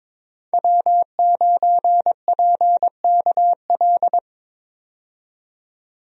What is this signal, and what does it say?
Morse code, heard as a keyed tone, about 21 wpm, 700 Hz. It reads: W9PKL
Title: USA Call Signs - 5 Characters - 22wpm